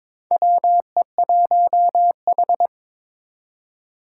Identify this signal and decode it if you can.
Morse code, heard as a keyed tone, about 22 wpm, 700 Hz. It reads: WE1H
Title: Four Challenging Letter-Number Combinations 22wpm